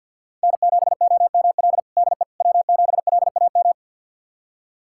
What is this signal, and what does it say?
Morse code, heard as a keyed tone, about 52 wpm, 690 Hz. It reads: N7OML DE W7LAM
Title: USA Callsign DE Callsign - 5 Characters - 50wpm